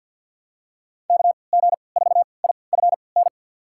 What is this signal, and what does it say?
Morse code, heard as a keyed tone, about 51 wpm, spaced as for 28 wpm, 700 Hz. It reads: KG4IFN
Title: USA Call Signs - 6 Characters - 3x Character Spacing - 50wpm